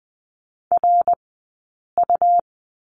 Morse code, heard as a keyed tone, about 20 wpm, spaced as for 8 wpm, 700 Hz. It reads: RU
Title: 171 Lesson 28d - Number 9 - Two Characters 20wpm